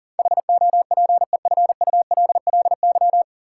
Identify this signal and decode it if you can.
Morse code, heard as a keyed tone, about 40 wpm, 700 Hz. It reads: HOPEFULLY
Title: Sentences from Top 1500 Words - New Word Drill Alt - 40wpm